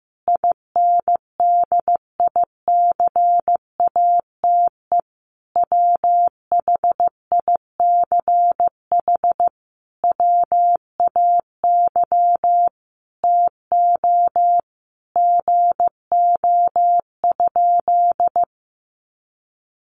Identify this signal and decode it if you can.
Morse code, heard as a keyed tone, about 15 wpm, 700 Hz. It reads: INDICATE WHICH WAY TO GO?
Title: Sentences from Top 1300 Words 15wpm